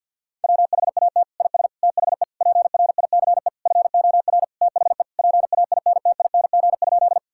Morse code, heard as a keyed tone, about 52 wpm, 700 Hz. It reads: WHAT IS THE PRICE FOR THE PAINTING?
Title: Sentences from Top 1000 Words - Review Part 6 - 50wpm